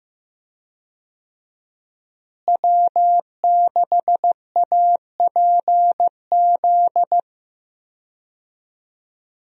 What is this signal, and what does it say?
Morse code, heard as a keyed tone, about 15 wpm, 700 Hz. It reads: W6APZ
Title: USA Call Signs - 5 Characters - Repeated 3x - 15wpm